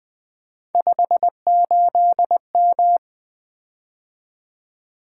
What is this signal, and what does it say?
Morse code, heard as a keyed tone, about 20 wpm, 700 Hz. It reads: 58M
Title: Three Letter-Number Combinations - 20wpm